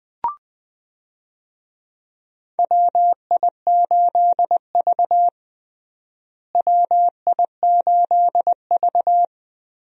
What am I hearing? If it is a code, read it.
Morse code, heard as a keyed tone, about 19 wpm, 700 Hz. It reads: WI8V WI8V
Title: USA Call Signs - 4 Characters - Repeated 3x - 20wpm